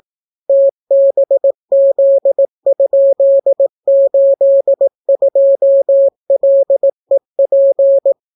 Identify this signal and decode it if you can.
Morse code, heard as a keyed tone, about 18 wpm, 550 Hz. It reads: TBZ?82LEP